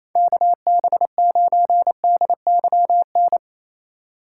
Morse code, heard as a keyed tone, about 28 wpm, 690 Hz. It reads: KB9DYN
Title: USA Call Signs - 3x Character Spacing - 28wpm